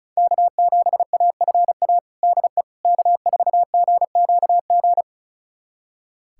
Morse code, heard as a keyed tone, about 36 wpm, 700 Hz. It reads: K7AFA DE K4GQG
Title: USA Callsign DE Callsign - 5 Characters - 35wpm